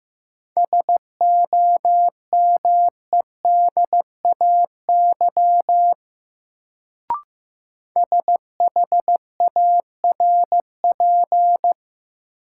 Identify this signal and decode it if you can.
Morse code, heard as a keyed tone, about 15 wpm, 700 Hz. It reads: SOMEDAY SHARP
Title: Sentences from Top 1600 Words - New Word Drill Alt - 15wpm